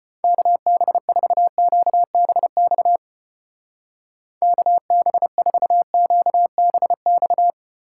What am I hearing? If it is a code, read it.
Morse code, heard as a keyed tone, about 29 wpm, 700 Hz. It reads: KB4QBX KB4QBX
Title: Speed Racing - USA 2x3 Call Signs 25wpm